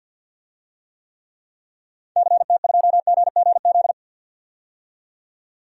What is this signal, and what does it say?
Morse code, heard as a keyed tone, about 51 wpm, 700 Hz. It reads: CT2GGZ
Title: DXCC-100 Call Signs - 6 Characters - 50wpm